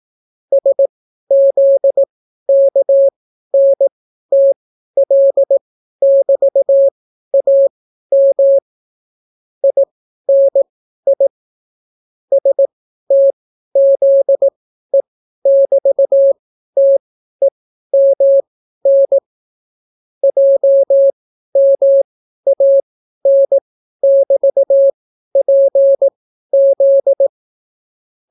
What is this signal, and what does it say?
Morse code, heard as a keyed tone, about 18 wpm, spaced as for 12 wpm, 550 Hz. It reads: SZKNTL<BT>AM INI STZE<BT>TEMN JMAN<BT>PZ